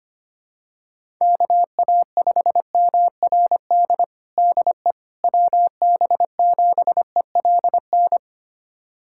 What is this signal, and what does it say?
Morse code, heard as a keyed tone, about 25 wpm, 700 Hz. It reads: KA5MRD DE WB7ELN
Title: USA Callsign DE Callsign 25wpm